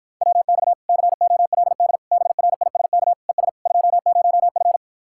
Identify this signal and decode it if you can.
Morse code, heard as a keyed tone, about 52 wpm, 700 Hz. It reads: WX COLD BRISK ES 20F